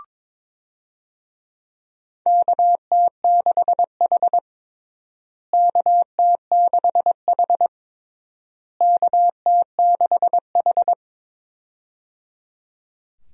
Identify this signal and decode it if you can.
Morse code, heard as a keyed tone, about 22 wpm, 700 Hz. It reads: KT6H KT6H KT6H
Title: USA Call Signs - 4 Characters - Repeated 3x - 22wpm